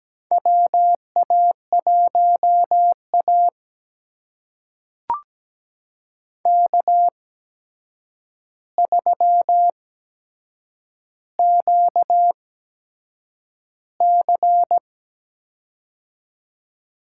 Morse code, heard as a keyed tone, about 17 wpm, spaced as for 7 wpm, 700 Hz. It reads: WA1A K3QC
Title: USA Call Signs - 4 Characters - 8x Character Spacing - 17wpm